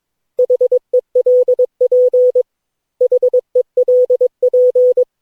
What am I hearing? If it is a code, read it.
Morse code, heard as a keyed tone, about 22 wpm, 500 Hz. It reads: HELP HELP
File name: cw / Session9 cw training